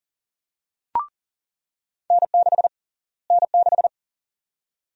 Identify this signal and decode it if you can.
Morse code, heard as a keyed tone, about 40 wpm, 700 Hz. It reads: N6 N6
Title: USA Call Sign Prefixes - Repeated 2x - 40wpm